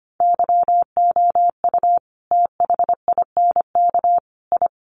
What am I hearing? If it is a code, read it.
Morse code, heard as a keyed tone, about 25 wpm, 700 Hz. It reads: YOU THINK I
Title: Sentences from Top 1200 Words 25wpm